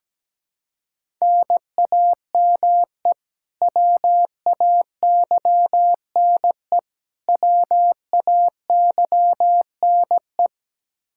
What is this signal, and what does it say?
Morse code, heard as a keyed tone, about 17 wpm, 700 Hz. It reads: NAME WAYNE WAYNE
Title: Practice QSO - Exchange 1 - Beginning - 17wpm